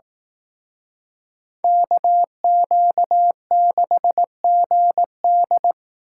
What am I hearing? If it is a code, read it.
Morse code, heard as a keyed tone, about 18 wpm, 700 Hz. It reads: KQ6GD
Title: Speed Racing - USA Call Signs 15wpm